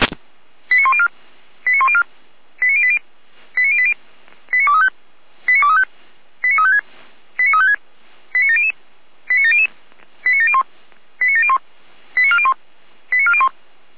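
Five-tone selective call (ZVEI), heard as quick runs of five tones, tones of about 70 ms. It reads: 89104 89104 89080 89080 89235 89235 89356 89356 89700 89700 89701 89701 89401 89401
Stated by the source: ZVEI five-tone call received by SDR.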